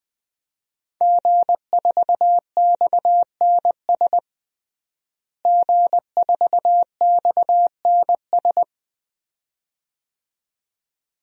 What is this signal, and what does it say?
Morse code, heard as a keyed tone, about 21 wpm, 700 Hz. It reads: G4XNS G4XNS
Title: DXCC-100 Call Signs - 5 Characters - Repeated 2x - 20wpm